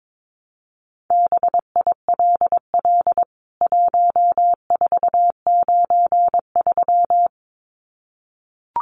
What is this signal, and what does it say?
Morse code, heard as a keyed tone, about 22 wpm, 700 Hz. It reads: BILL 1493
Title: CWT Exchange 22wpm